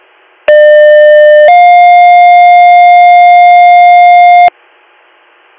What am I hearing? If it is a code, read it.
Two-tone page, heard as a one-second tone, then a three-second tone, about 600 then 710 Hz. It is heard from 0.5 to 4.5 s.